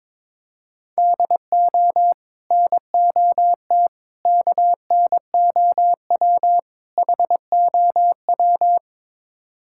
Morse code, heard as a keyed tone, about 22 wpm, 700 Hz. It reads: DO NOT KNOW HOW
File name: practice / 22wpm / Build A Phrase - 5-Word Phrases - 22wpm